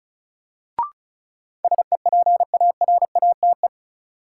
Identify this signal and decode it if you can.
Morse code, heard as a keyed tone, about 34 wpm, 700 Hz. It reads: SEPARATE
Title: Sentences from Top 1000 Words - Review Part 7 - New Word Drill - 35wpm